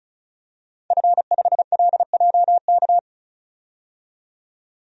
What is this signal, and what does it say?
Morse code, heard as a keyed tone, about 34 wpm, 700 Hz. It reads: F5LJK
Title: DXCC-100 Call Signs - 5 Characters - 35wpm